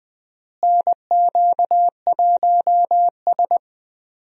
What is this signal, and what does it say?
Morse code, heard as a keyed tone, about 20 wpm, 700 Hz. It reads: NQ1S
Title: USA Call Signs - 4 Characters - Repeated 2x - 20wpm